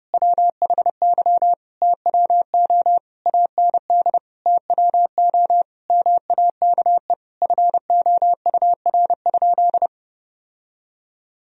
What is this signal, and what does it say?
Morse code, heard as a keyed tone, about 28 wpm, 700 Hz. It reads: WHY TWO AND TWO MAKE FOUR?
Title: Sentences from Top 1000 Words - Review Part 1 - 30wpm